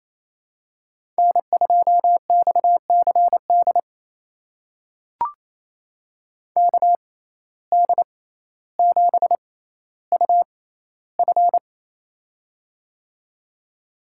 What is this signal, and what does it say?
Morse code, heard as a keyed tone, about 28 wpm, spaced as for 14 wpm, 700 Hz. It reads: N2XCD KD7UF
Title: USA Call Signs - 5 Characters - 6x Character Spacing - 28wpm